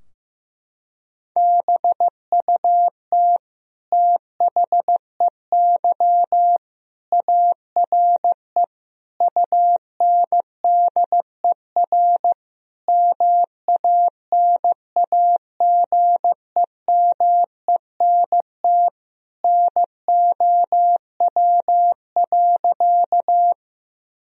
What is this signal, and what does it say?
Morse code, heard as a keyed tone, about 15 wpm, 700 Hz. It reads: BUT THEY ARE UNDER MANAGEMENT NOW.
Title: Sentences from Top 1000 Words - Review Part 4 - 15wpm